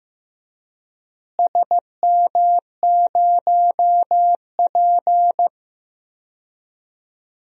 Morse code, heard as a keyed tone, about 15 wpm, 700 Hz. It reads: SM0P
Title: DXCC-100 Call Signs - 4 Characters - Repeated 2x - 15wpm